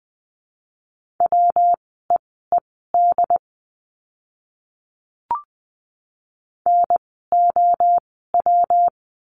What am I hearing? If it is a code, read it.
Morse code, heard as a keyed tone, about 20 wpm, spaced as for 15 wpm, 700 Hz. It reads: WEED NOW
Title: 115 Lesson 19f - Letter w - Words 20wpm